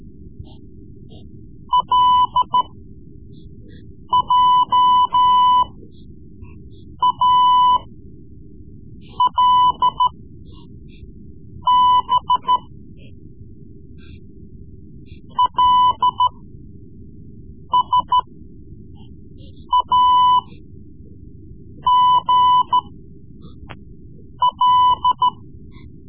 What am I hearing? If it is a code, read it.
Morse code, heard as a keyed tone, about 13 wpm, spaced as for 5 wpm, 1000 Hz. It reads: LJALB LSAGL
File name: Morse Lessons / Morse Lessons / Tape-3B